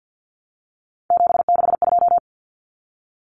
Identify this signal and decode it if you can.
Morse code, heard as a keyed tone, about 50 wpm, 700 Hz. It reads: 762
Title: Common Cut Numbers - 3-Digit 50wpm